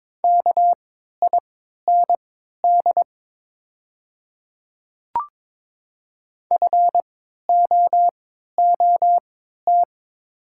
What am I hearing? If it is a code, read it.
Morse code, heard as a keyed tone, about 22 wpm, spaced as for 12 wpm, 700 Hz. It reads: KIND FOOT
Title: 3x Character Spacing - Top 500 Words - 22wpm